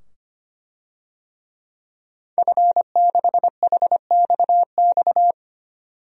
Morse code, heard as a keyed tone, about 25 wpm, 700 Hz. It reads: F6HXX
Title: DXCC-100 Call Signs - 5 Characters - 25wpm